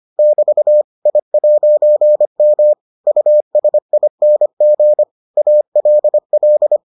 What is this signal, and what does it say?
Morse code, heard as a keyed tone, about 25 wpm, 600 Hz. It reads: <BT> I'M USING ALL